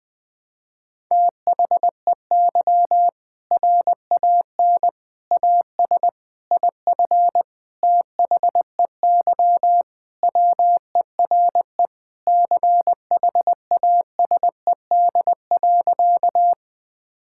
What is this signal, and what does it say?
Morse code, heard as a keyed tone, about 20 wpm, 700 Hz. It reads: THEY RAN AS IF THEY WERE CHASED.
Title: Sentences with 4-Word Phrases - 20wpm